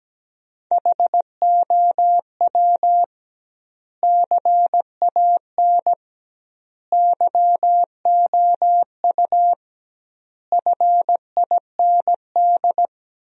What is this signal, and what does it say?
Morse code, heard as a keyed tone, about 17 wpm, 700 Hz. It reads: HOW CAN YOU FIND
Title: 2x Word-Spacing - Sentences from Top 200 Words - 17wpm